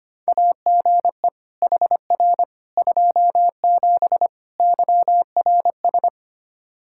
Morse code, heard as a keyed tone, about 25 wpm, 700 Hz. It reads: AGE HR 27 YRS
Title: Practice QSO - Exchange 3 - More Info - 25wpm